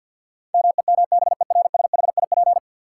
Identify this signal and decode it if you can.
Morse code, heard as a keyed tone, about 53 wpm, 700 Hz. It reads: MEMBERSHIP
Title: Sentences from Top 1400 Words - New Word Drill - 50wpm